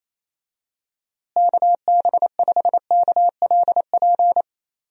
Morse code, heard as a keyed tone, about 28 wpm, 700 Hz. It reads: KB5KLP
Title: USA Call Signs - 6 Characters - Repeated 2x - 28wpm